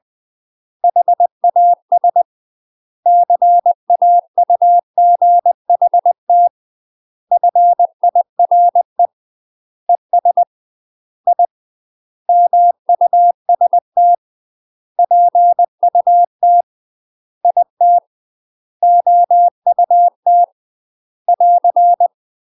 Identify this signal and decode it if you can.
Morse code, heard as a keyed tone, about 20 wpm, 700 Hz. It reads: HAS CAUGHT FIRE ES I MUST PUT IT OUT <AR>
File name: UK MASTER QSOs TRAINING / 20WPM / UK MASTER QSOs 0017